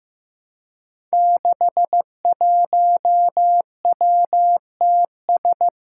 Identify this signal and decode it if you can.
Morse code, heard as a keyed tone, about 15 wpm, 700 Hz. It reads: 61WTS
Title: Five Challenging Letter-Number Combinations 15wpm